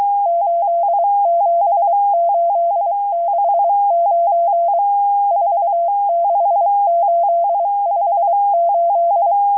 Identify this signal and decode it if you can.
Morse code, heard as a keyed tone, about 23 wpm, 680 Hz, with radio noise. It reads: 87869 46858